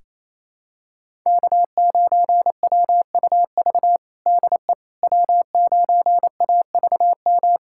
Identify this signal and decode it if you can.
Morse code, heard as a keyed tone, about 28 wpm, 690 Hz. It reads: K9WUV DE W9AVM
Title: USA Callsign DE Callsign - 5 Characters - 28wpm